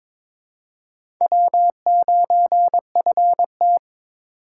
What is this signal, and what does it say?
Morse code, heard as a keyed tone, about 22 wpm, 700 Hz. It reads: W9FT
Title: USA Call Signs - 4 Characters - 2x Character Spacing - 22wpm